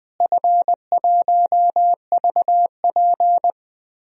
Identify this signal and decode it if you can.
Morse code, heard as a keyed tone, about 20 wpm, 700 Hz. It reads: F1VP
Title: DXCC-100 Call Signs - 4 Characters - Repeated 3x - 20wpm